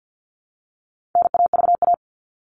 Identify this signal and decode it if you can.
Morse code, heard as a keyed tone, about 48 wpm, 700 Hz. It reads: NAVA
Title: All Cut Numbers - 4-Digit 50wpm